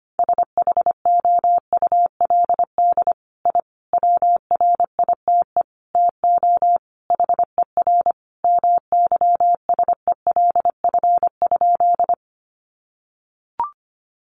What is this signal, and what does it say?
Morse code, heard as a keyed tone, about 25 wpm, 700 Hz. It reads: SHOULD I WRITE TO HER MYSELF?
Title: Sentences from Top 1300 Words 25wpm